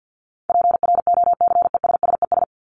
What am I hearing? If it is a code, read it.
Morse code, heard as a keyed tone, about 50 wpm, 700 Hz. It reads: PROCESSES
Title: ES:S Suffix - Top 100 Words with Base Words 50wpm